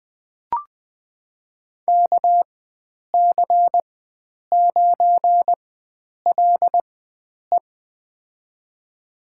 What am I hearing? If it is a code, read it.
Morse code, heard as a keyed tone, about 20 wpm, spaced as for 9 wpm, 700 Hz. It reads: KC9LE
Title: USA Call Signs - 5 Characters - 4x Character Spacing - 20wpm